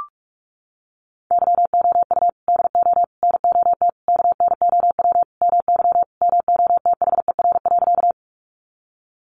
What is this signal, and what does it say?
Morse code, heard as a keyed tone, about 44 wpm, 700 Hz. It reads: YOU DO NOT KNOW MY MOTHER.
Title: Sentences from Top 600 Words 45wpm